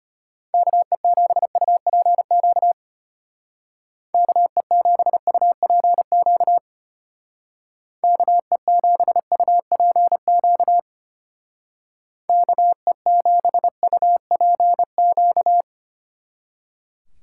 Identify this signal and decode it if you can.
Morse code, heard as a keyed tone, about 32 wpm, 700 Hz. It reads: KE7UPQ KE7UPQ KE7UPQ KE7UPQ